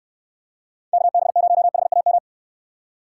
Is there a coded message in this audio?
Yes: Morse code from 0.9 to 2.2 s, heard as a keyed tone, about 64 wpm, 700 Hz. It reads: KB0HAM